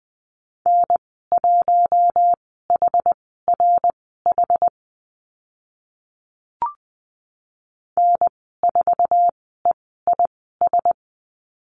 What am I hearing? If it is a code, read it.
Morse code, heard as a keyed tone, about 20 wpm, spaced as for 14 wpm, 700 Hz. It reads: N1HRH N4EIS
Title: 062 Lesson 11g - Letter h - Call Signs 20wpm